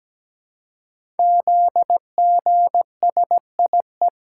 Morse code, heard as a keyed tone, about 17 wpm, 700 Hz. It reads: ZGSIE
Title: Code Groups - 5 Letters - 17wpm